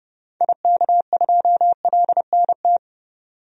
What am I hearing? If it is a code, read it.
Morse code, heard as a keyed tone, about 32 wpm, 700 Hz. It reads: IK2LNT
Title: DXCC-100 Call Signs - 6 Characters - 30wpm